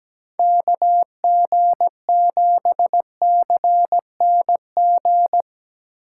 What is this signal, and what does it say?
Morse code, heard as a keyed tone, about 17 wpm, 700 Hz. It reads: KG7CNG
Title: USA Call Signs - 6 Characters - 8x Character Spacing - 17wpm